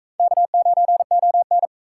Morse code, heard as a keyed tone, about 43 wpm, 700 Hz. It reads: K9ON